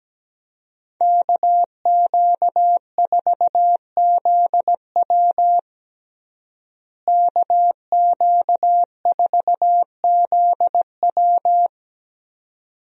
Morse code, heard as a keyed tone, about 17 wpm, 700 Hz. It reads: KQ4ZW KQ4ZW
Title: USA Call Signs - 5 Characters - Repeated 2x - 17wpm